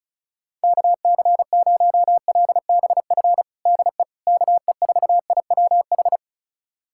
Morse code, heard as a keyed tone, about 34 wpm, 700 Hz. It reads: KC0LBF DE KE4IWH
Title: USA Callsign DE Callsign - 6 Characters - 35wpm